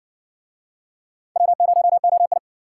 Morse code, heard as a keyed tone, about 62 wpm, 700 Hz. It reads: W0OI